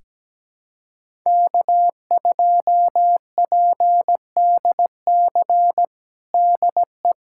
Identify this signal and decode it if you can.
Morse code, heard as a keyed tone, about 17 wpm, 700 Hz. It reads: K2PDC DE
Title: USA Callsign DE Callsign - 5 Characters - 17wpm